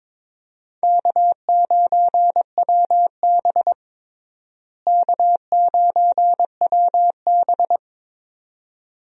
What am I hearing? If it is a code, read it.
Morse code, heard as a keyed tone, about 22 wpm, 700 Hz. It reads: K9WB K9WB
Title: USA Call Signs - 4 Characters - Repeated 2x - 22wpm